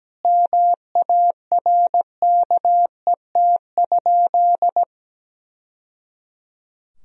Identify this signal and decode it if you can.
Morse code, heard as a keyed tone, about 17 wpm, 700 Hz. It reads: MARKET?
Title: Sentences from Top 1000 Words - Review Part 3 - 17wpm